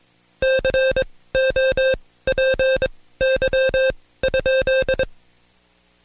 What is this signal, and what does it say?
Morse code, heard as a keyed tone, about 21 wpm, spaced as for 16 wpm, 530 Hz. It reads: COPY?
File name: Code Quick / AudioDownld2 / Seg134-Stepper7 15WPM